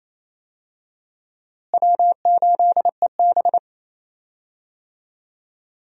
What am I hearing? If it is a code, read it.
Morse code, heard as a keyed tone, about 28 wpm, 700 Hz. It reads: W8EB
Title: USA Call Signs - 4 Characters - Repeated 3x - 28wpm